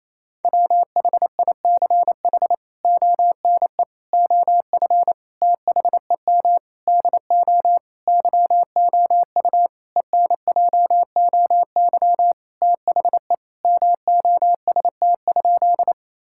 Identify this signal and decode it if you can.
Morse code, heard as a keyed tone, about 28 wpm, 700 Hz. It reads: WHICH ONE OF THEM DO YOU ENJOY THE MOST?
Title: Sentences from Top 1000 Words - Review Part 5 - 28wpm